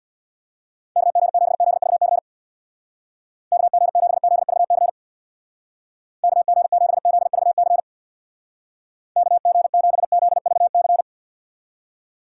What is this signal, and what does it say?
Morse code, heard as a keyed tone, about 62 wpm, 700 Hz. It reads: KG7ZVC KG7ZVC KG7ZVC KG7ZVC
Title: Speed Racing - USA 2x3 Call Signs 50wpm